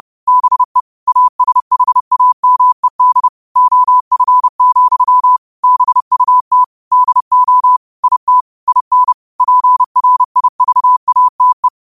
Morse code, heard as a keyed tone, about 30 wpm, 1000 Hz. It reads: BE ASHAMED OF, BUT DO IT IN PRIVATE